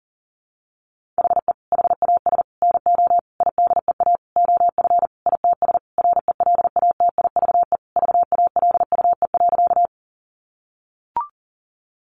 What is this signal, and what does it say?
Morse code, heard as a keyed tone, about 40 wpm, 700 Hz. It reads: HE HAS NO IDEA OF ITS RELATIVE VALUE.
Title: Sentences from Top 1400 Words 40wpm